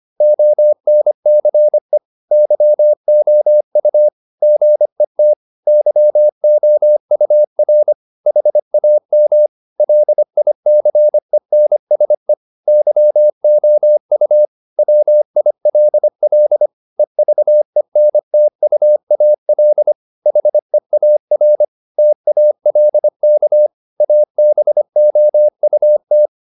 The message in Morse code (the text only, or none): ONCE YOU GET YOUR HAM LICENSE YOU WILL EVENTUAL HEAR TALK ABOUT